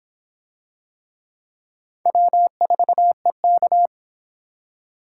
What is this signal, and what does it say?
Morse code, heard as a keyed tone, about 24 wpm, 700 Hz. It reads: W4EK